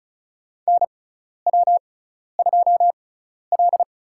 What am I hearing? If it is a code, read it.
Morse code, heard as a keyed tone, about 34 wpm, spaced as for 12 wpm, 700 Hz. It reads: NW2L
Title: USA Call Signs - 6x Character Spacing - 35wpm